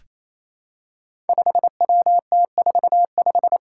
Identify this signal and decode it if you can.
Morse code, heard as a keyed tone, about 28 wpm, 700 Hz. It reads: 5WT45
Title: Five Challenging Letter-Number Combinations 28wpm